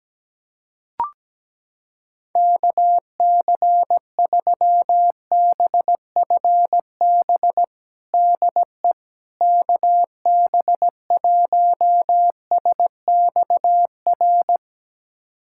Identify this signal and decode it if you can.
Morse code, heard as a keyed tone, about 17 wpm, 700 Hz. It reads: KC3BFB DE KB1SXR
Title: USA Callsign DE Callsign - 6 Characters - 17wpm